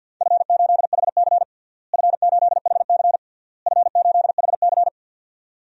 Morse code, heard as a keyed tone, about 51 wpm, 700 Hz. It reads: F8HC F8HC F8HC
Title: DXCC-100 Call Signs - 4 Characters - Repeated 3x - 50wpm